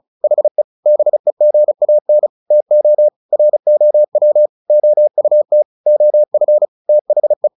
Morse code, heard as a keyed tone, about 34 wpm, 600 Hz. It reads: HE BEGAN TO ROW OUT OF THE
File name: the old man and the sea / the old man and the sea 1 35wpm